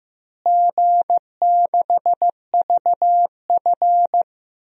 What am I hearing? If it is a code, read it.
Morse code, heard as a keyed tone, about 15 wpm, 700 Hz. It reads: G6VF
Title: DXCC-100 Call Signs - 4 Characters - Repeated 3x - 15wpm